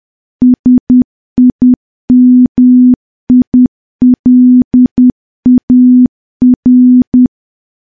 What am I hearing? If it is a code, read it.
Morse code, heard as a keyed tone, about 10 wpm, 250 Hz. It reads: SIMILAR
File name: morse / morse-training-10WPM